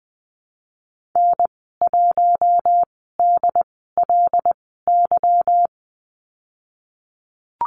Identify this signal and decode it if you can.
Morse code, heard as a keyed tone, about 20 wpm, spaced as for 14 wpm, 700 Hz. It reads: N1DLY
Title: 186 Lesson 30g - Letter b - Call Signs 20wpm